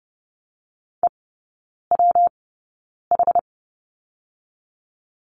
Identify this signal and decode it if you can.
Morse code, heard as a keyed tone, about 30 wpm, 700 Hz. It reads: E W H
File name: Lesson 19e - Letter w - Three Characters 30wpm